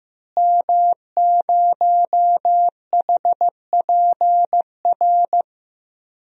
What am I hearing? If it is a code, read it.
Morse code, heard as a keyed tone, about 16 wpm, 700 Hz. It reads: M0HPR
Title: DXCC-100 Call Signs - 5 Characters - Repeated 2x - 15wpm